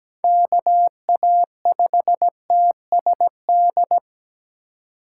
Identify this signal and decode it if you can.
Morse code, heard as a keyed tone, about 20 wpm, 700 Hz. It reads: KA5TSD
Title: Speed Racing - USA 2x3 Call Signs 17wpm